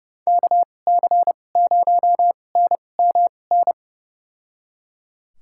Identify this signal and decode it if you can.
Morse code, heard as a keyed tone, about 30 wpm, spaced as for 22 wpm, 700 Hz. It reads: KC0NMN
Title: USA Call Signs - 6 Characters - 2x Character Spacing - 30wpm